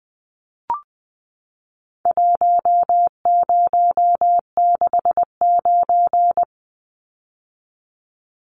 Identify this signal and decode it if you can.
Morse code, heard as a keyed tone, about 20 wpm, 700 Hz. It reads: 1069
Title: Numbers - Contest Style - 4-Digit 20wpm